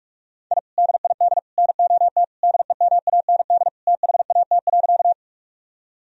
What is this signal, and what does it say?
Morse code, heard as a keyed tone, about 45 wpm, 700 Hz. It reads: I DID NOT DEMAND THAT.
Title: Sentences from Top 1000 Words - Review Part 4 - 45wpm